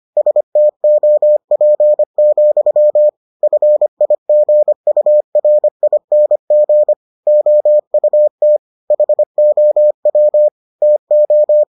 Morse code, heard as a keyed tone, about 25 wpm, 600 Hz. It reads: STOP, FIGURING OUT HOW TO